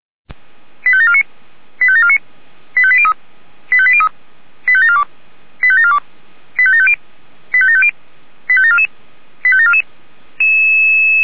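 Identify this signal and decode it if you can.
Five-tone selective call (ZVEI), heard as quick runs of five tones, tones of about 70 ms. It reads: 85749 85749 85803 85803 85742 85742 85750 85750 85744 85744
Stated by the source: ZVEI five-tone call received by SDR.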